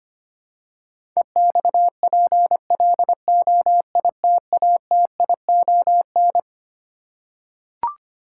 Morse code, heard as a keyed tone, about 25 wpm, 700 Hz. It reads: EXPLOITATION
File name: Twelve Letter Words - 25wpm